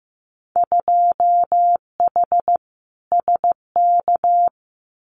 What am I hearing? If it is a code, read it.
Morse code, heard as a keyed tone, about 15 wpm, 700 Hz. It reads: 2H SK
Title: Winter Field Day Exchanges Encore 15wpm